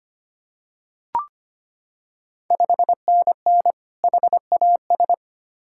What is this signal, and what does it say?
Morse code, heard as a keyed tone, about 25 wpm, 700 Hz. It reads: 5NN HAS
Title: Ontario QSO Party Exchange 25wpm